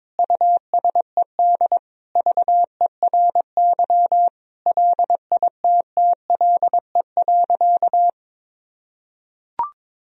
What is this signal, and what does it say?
Morse code, heard as a keyed tone, about 22 wpm, 700 Hz. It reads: USED VERY LITTLE.
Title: Sentences from Top 1000 Words - Review Part 3 - 22wpm